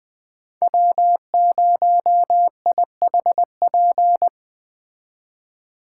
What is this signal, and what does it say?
Morse code, heard as a keyed tone, about 20 wpm, 700 Hz. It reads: W0IHP